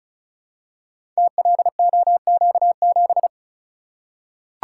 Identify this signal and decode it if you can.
Morse code, heard as a keyed tone, about 36 wpm, 700 Hz. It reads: TLOQ7
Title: Code Groups - 5 Letter:Numbers - 35wpm